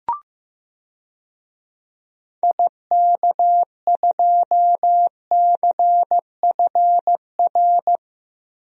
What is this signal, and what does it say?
Morse code, heard as a keyed tone, about 15 wpm, 700 Hz. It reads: IK2CFR
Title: DXCC-100 Call Signs - 6 Characters - Repeated 2x - 15wpm